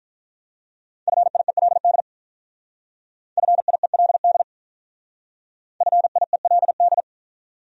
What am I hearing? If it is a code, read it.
Morse code, heard as a keyed tone, about 47 wpm, 700 Hz. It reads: FIELD FIELD FIELD